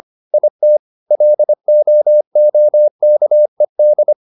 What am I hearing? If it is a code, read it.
Morse code, heard as a keyed tone, about 25 wpm, 600 Hz. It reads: IT LOOKED